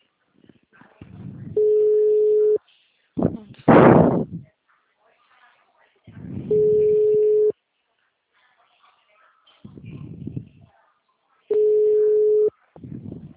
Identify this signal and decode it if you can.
Ringback tone (Europe, 425 Hz).
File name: out- -2011- - - .29597